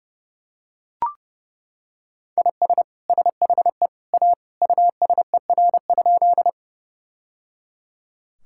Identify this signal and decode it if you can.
Morse code, heard as a keyed tone, about 30 wpm, 700 Hz. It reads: IS SHE A USER?
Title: Sentences from Top 1000 Words - Review Part 5 - 30wpm